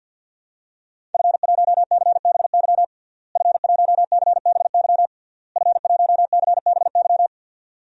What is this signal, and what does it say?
Morse code, heard as a keyed tone, about 53 wpm, 700 Hz. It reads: F1CBY F1CBY F1CBY
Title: DXCC-100 Call Signs - 5 Characters - Repeated 3x - 50wpm